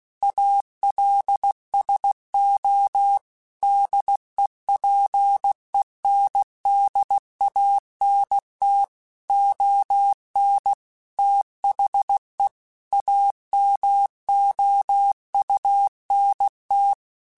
Morse code, heard as a keyed tone, about 16 wpm, 800 Hz. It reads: ALSO DEPENDANT ON THE AMOUNT
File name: Technical Subjects / Technical Subjects-mp3s / XL @15